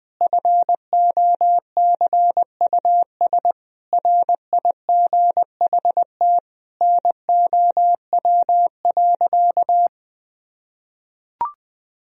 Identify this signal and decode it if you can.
Morse code, heard as a keyed tone, about 20 wpm, 700 Hz. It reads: FOCUS RIGHT NOW.
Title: Sentences from Top 1000 Words - Review Part 4 - 20wpm